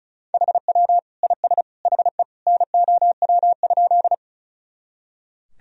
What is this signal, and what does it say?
Morse code, heard as a keyed tone, about 34 wpm, 700 Hz. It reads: HW IS HE NOW?
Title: Sentences from Top 100 Words and Abbreviations 35wpm